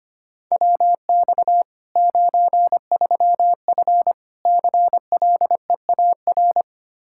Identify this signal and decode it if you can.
Morse code, heard as a keyed tone, about 25 wpm, 700 Hz. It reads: WX 93F CLEAR
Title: Practice QSO - Exchange 2 - Info - 25wpm